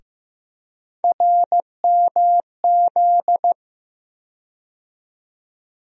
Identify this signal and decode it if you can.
Morse code, heard as a keyed tone, about 16 wpm, 700 Hz. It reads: RMZ
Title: USA Call Sign Postfixes - 3 Letters 15wpm